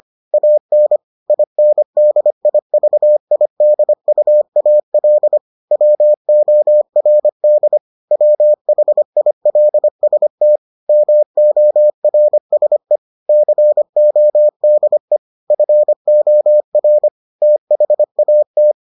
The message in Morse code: AN INDIVIDUAL WORD WHILST MORSE CODE FOR THAT

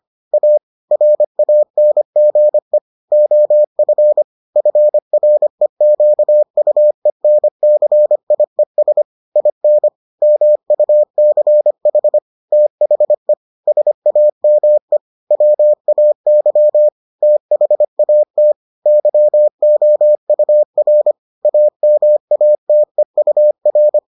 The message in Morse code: A RANGE OF FREQUENCIES IN MUCH THE SAME WAY THAT YOUR AMATEUR

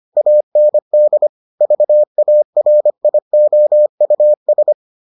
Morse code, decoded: AND VARIOUS

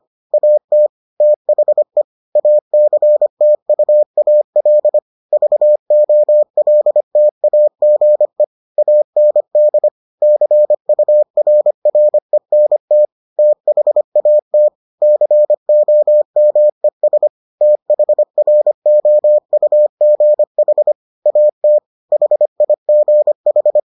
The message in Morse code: AT THE ACTUAL VOLTAGE AND CURRENT THAT COMES THROUGH AT HIGH

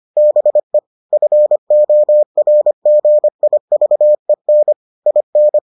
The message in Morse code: BE FORGIVEN IN

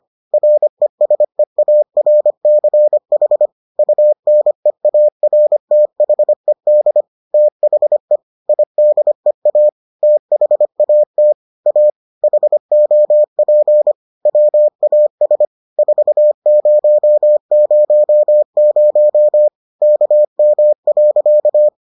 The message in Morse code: RESEARCH UNEARTHED THE IDEA THAT A HOP WAS 4000 KM.

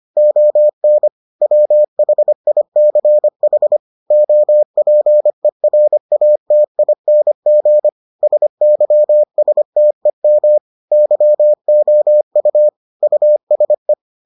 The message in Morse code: ON WHICH OPERATING SYSTEM YOU USE

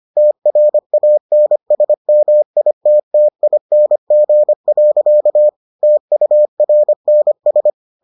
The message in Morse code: TRANSMITTING. TURNS

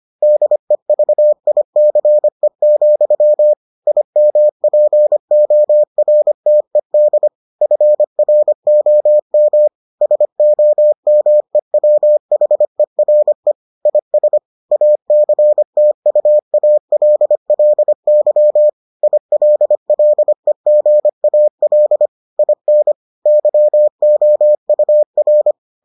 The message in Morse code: DEVICE, IMPORTED FROM SOMEWHERE IS ACTUALLY ILLEGAL IN YOUR

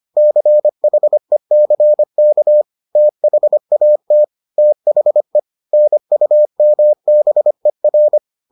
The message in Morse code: CHECK THAT THE NUMBER

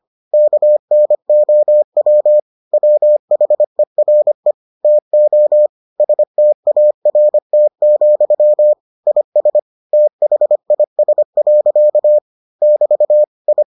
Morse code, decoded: KNOW WHERE TO START, IS THIS. <BT> I

